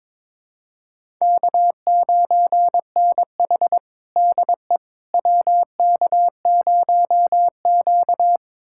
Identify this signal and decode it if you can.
Morse code, heard as a keyed tone, about 21 wpm, 700 Hz. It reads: K9NH DE WK0Q
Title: USA Callsign DE Callsign - 4 Characters - 22wpm